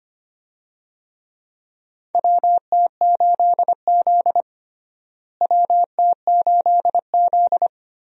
Morse code, heard as a keyed tone, about 25 wpm, 700 Hz. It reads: WT8Z WT8Z
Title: USA Call Signs - 4 Characters - Repeated 3x - 25wpm